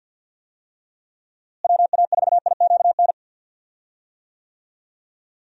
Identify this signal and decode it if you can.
Morse code, heard as a keyed tone, about 46 wpm, 700 Hz. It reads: WA4IQN
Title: USA Call Signs - 6 Characters - Repeated 3x - 50wpm